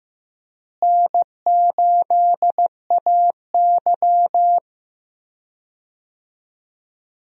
Morse code, heard as a keyed tone, about 15 wpm, 700 Hz. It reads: N8AY
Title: USA Call Signs - 4 Characters - Repeated 3x - 15wpm